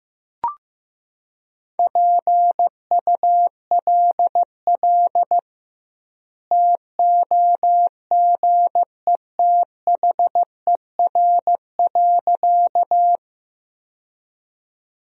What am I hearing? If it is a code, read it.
Morse code, heard as a keyed tone, about 16 wpm, 700 Hz. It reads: PULL TOGETHER.
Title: 2x Word-Spacing - Sentences from Top 400 Words - 15wpm